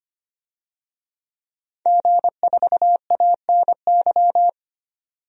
Morse code, heard as a keyed tone, about 25 wpm, 700 Hz. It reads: G4ANY
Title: DXCC-100 Call Signs - 5 Characters - 25wpm